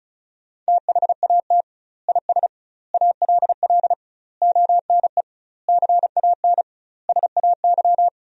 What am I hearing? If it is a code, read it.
Morse code, heard as a keyed tone, about 36 wpm, 700 Hz. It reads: THAT IS ALL ONE CAN SAY
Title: 2x Word-Spacing - Sentences from Top 200 Words - 35wpm